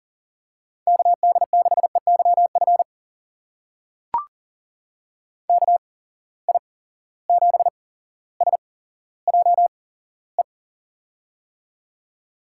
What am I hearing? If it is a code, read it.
Morse code, heard as a keyed tone, about 40 wpm, spaced as for 17 wpm, 700 Hz. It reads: KD6EYF KI7SJE
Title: USA Call Signs - 6 Characters - 8x Character Spacing - 40wpm